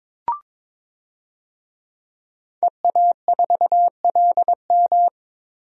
Morse code, heard as a keyed tone, about 22 wpm, 700 Hz. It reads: EA4LM
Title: DXCC-100 Call Signs - 5 Characters - Repeated 3x - 22wpm